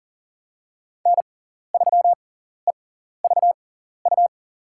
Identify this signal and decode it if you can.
Morse code, heard as a keyed tone, about 40 wpm, spaced as for 14 wpm, 690 Hz. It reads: N3EVU
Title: USA Call Signs - 5 Characters - 6x Character Spacing - 40wpm